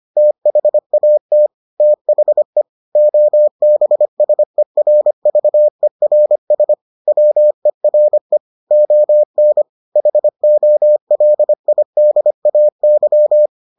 Morse code, heard as a keyed tone, about 25 wpm, 600 Hz. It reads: THAT THE OBSERVERS WERE ON HOLIDAY